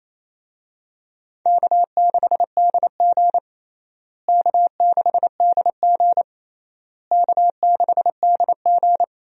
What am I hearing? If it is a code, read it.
Morse code, heard as a keyed tone, about 28 wpm, 700 Hz. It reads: K6DG K6DG K6DG
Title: USA Call Signs - 4 Characters - Repeated 3x - 28wpm